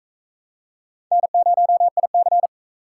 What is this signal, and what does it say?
Morse code, heard as a keyed tone, about 40 wpm, 700 Hz. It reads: N0IC